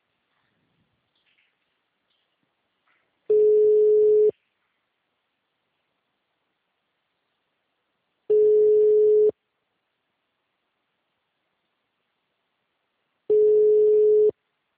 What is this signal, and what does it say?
Ringback tone (Europe, 425 Hz).
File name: out- -2007- - - .21